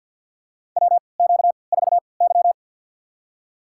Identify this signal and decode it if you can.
Morse code, heard as a keyed tone, about 52 wpm, spaced as for 29 wpm, 700 Hz. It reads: WQ4Y
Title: USA Call Signs - 4 Characters - 3x Character Spacing - 50wpm